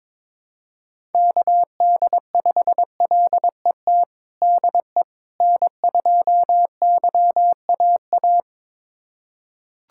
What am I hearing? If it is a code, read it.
Morse code, heard as a keyed tone, about 21 wpm, 700 Hz. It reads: KD5LET DE N2YAA
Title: USA Callsign DE Callsign 22wpm